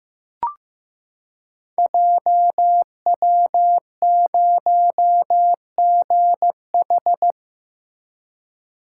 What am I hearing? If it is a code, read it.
Morse code, heard as a keyed tone, about 15 wpm, 700 Hz. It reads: JW0GH
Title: Code Groups - 5 Letter:Numbers - 15wpm